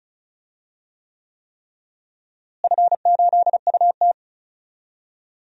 Morse code, heard as a keyed tone, about 36 wpm, 700 Hz. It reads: F8UT
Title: DXCC-100 Call Signs - 4 Characters - Repeated 3x - 35wpm